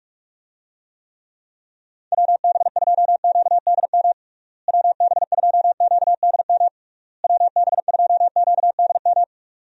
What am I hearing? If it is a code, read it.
Morse code, heard as a keyed tone, about 45 wpm, 700 Hz. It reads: WB2QDM WB2QDM WB2QDM
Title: USA Call Signs - 6 Characters - Repeated 3x - 45wpm